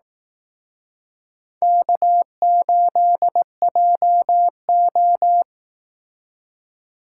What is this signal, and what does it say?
Morse code, heard as a keyed tone, about 18 wpm, 700 Hz. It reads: K8JO
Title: Speed Racing - USA 1x2 Call Signs 15wpm